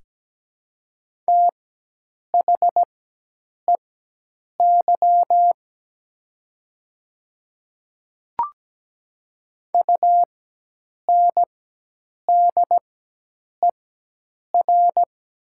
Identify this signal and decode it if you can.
Morse code, heard as a keyed tone, about 17 wpm, spaced as for 8 wpm, 700 Hz. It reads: THEY UNDER
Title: 4x Character Spacing - Top 300 Words - 17wpm